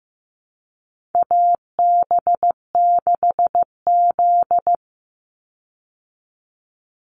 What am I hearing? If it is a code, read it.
Morse code, heard as a keyed tone, about 15 wpm, 700 Hz. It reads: AB6Z
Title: Call Signs Repeated 3x - 15wpm